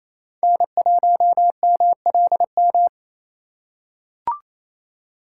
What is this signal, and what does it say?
Morse code, heard as a keyed tone, about 28 wpm, 700 Hz. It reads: N1MLM
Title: USA Call Signs - 4x Character Spacing - 28wpm